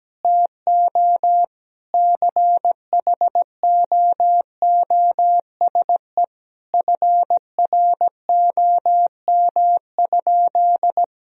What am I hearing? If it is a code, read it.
Morse code, heard as a keyed tone, about 17 wpm, 700 Hz. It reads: TO CHOOSE FROM?
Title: Sentences from Top 1000 Words - Review Part 4 - 17wpm